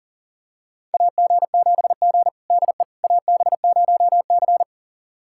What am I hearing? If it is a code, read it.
Morse code, heard as a keyed tone, about 40 wpm, 700 Hz. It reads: AG7G DE AB0C
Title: USA Callsign DE Callsign - 4 Characters - 40wpm